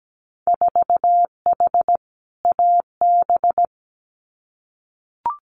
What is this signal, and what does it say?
Morse code, heard as a keyed tone, about 17 wpm, 700 Hz. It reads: 4H AB
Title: Winter Field Day Exchanges Encore 17wpm